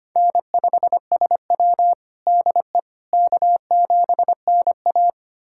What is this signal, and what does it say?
Morse code, heard as a keyed tone, about 25 wpm, 700 Hz. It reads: N5SW DE K7NA
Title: USA Callsign DE Callsign - 4 Characters - 25wpm